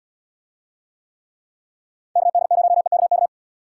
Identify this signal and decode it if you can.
Morse code, heard as a keyed tone, about 70 wpm, 700 Hz. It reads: KD0EKK